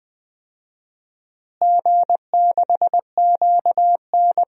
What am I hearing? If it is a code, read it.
Morse code, heard as a keyed tone, about 20 wpm, 700 Hz. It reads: G6QN
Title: DXCC-100 Call Signs - 4 Characters - Repeated 3x - 20wpm